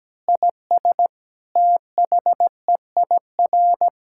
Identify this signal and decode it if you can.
Morse code, heard as a keyed tone, about 17 wpm, 700 Hz. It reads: IS THEIR
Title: Sentences from Top 1000 Words - Review Part 5 - 17wpm